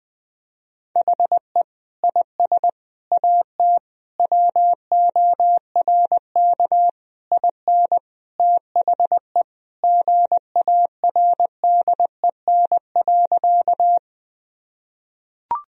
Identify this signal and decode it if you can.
Morse code, heard as a keyed tone, about 21 wpm, 700 Hz. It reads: HE IS AT WORK IN THE GARDEN.
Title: Sentences from Top 1000 Words - Review Part 4 - 20wpm